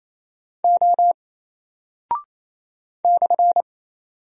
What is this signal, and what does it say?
Morse code, heard as a keyed tone, about 27 wpm, 700 Hz. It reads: O /
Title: Single Character (Letter:Number:Punctuation:Prosign) 28wpm